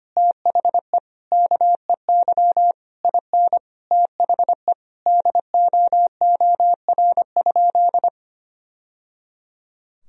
Morse code, heard as a keyed tone, about 25 wpm, 700 Hz. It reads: THE KEY IN THE DOOR?
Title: Sentences from Top 1000 Words - Review Part 4 - 25wpm